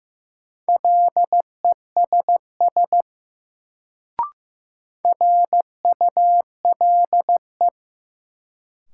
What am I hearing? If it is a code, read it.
Morse code, heard as a keyed tone, about 15 wpm, 700 Hz. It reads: LESS RULE
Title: Sentences from Top 400 Words - New Word Drill - 15wpm